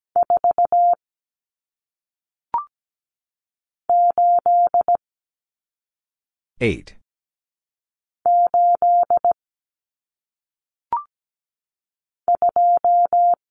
Morse code, heard as a keyed tone, about 17 wpm, 700 Hz. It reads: 4 8 8 2
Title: Numbers - 1-Digit 17wpm